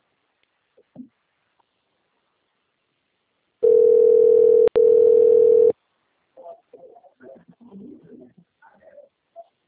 Ringback tone (North America), 440 + 480 Hz.